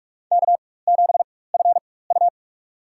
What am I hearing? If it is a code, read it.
Morse code, heard as a keyed tone, about 45 wpm, spaced as for 21 wpm, 700 Hz. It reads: K7FU
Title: USA Call Signs - 4 Characters - 4x Character Spacing - 45wpm